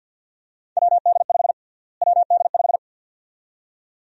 Morse code, heard as a keyed tone, about 49 wpm, 700 Hz. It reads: WD5 WD5
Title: USA Call Sign Prefixes - Repeated 2x - 50wpm